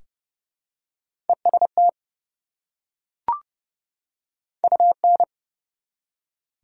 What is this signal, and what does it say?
Morse code, heard as a keyed tone, about 30 wpm, 700 Hz. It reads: EST UN
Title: Prefixes and Suffixes 30wpm